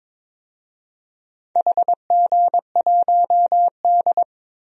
Morse code, heard as a keyed tone, about 22 wpm, 700 Hz. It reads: HG1D
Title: DXCC-100 Call Signs - 4 Characters - 22wpm